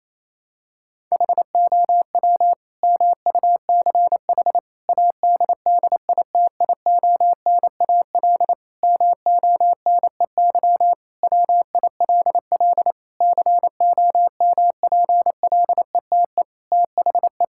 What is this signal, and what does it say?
Morse code, heard as a keyed tone, about 28 wpm, 700 Hz. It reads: HOW MUCH ADDITIONAL MONEY WILL COMPLETE THE
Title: Sentences from Top 1000 Words - Review Part 5 - 28wpm